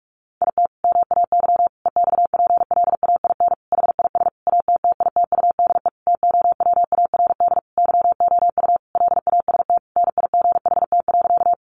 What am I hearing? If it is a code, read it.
Morse code, heard as a keyed tone, about 44 wpm, 700 Hz. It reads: IT MAY EXPLAIN HIS ATTITUDE TOWARD YOU LAST NIGHT.
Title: Sentences from Top 1300 Words 45wpm